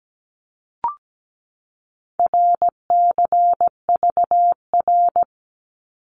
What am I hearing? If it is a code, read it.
Morse code, heard as a keyed tone, about 17 wpm, 700 Hz. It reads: RCVR
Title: Abbreviations 17wpm